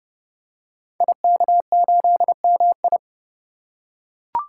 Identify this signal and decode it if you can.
Morse code, heard as a keyed tone, about 30 wpm, 700 Hz. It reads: IK8MI
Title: Code Groups - 5 Letter:Numbers - 30wpm